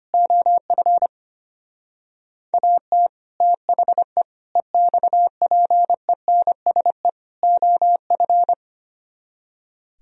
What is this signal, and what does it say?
Morse code, heard as a keyed tone, about 32 wpm, 700 Hz. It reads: OF AT THE EXPENSE OF